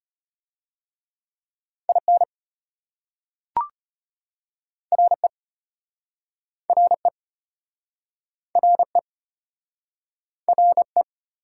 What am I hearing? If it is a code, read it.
Morse code, heard as a keyed tone, about 33 wpm, 700 Hz. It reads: IN RE RE RE RE